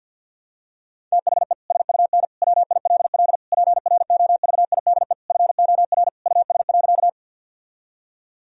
Morse code, heard as a keyed tone, about 50 wpm, 700 Hz. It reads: THE SUN WILL PROVIDE FOR US.
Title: Sentences from Top 1000 Words - Review Part 3 - 50wpm